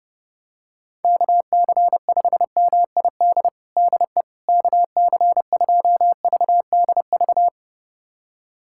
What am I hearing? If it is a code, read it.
Morse code, heard as a keyed tone, about 30 wpm, 700 Hz. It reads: KC5MID DE KC2VDV
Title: USA Callsign DE Callsign - 6 Characters - 30wpm